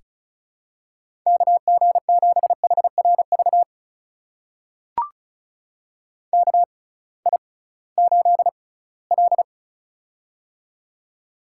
Morse code, heard as a keyed tone, about 36 wpm, spaced as for 20 wpm, 700 Hz. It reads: KG7HRV KI8L
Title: USA Call Signs - 6x Character Spacing - 35wpm